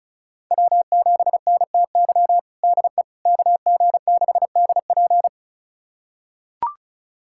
Morse code, heard as a keyed tone, about 36 wpm, 700 Hz. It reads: W7NTY DE KG6DP
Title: USA Callsign DE Callsign - 5 Characters - 35wpm